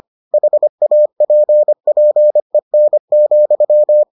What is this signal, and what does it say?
Morse code, heard as a keyed tone, about 25 wpm, 600 Hz. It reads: HAPPEN,